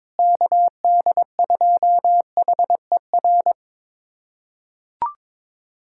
Morse code, heard as a keyed tone, about 22 wpm, 700 Hz. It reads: KD2HER
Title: USA Callsign DE Callsign - 6 Characters - 22wpm